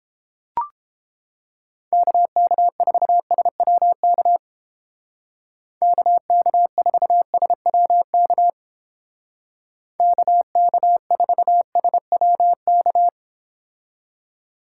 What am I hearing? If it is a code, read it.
Morse code, heard as a keyed tone, about 29 wpm, 700 Hz. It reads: KK4SWK KK4SWK KK4SWK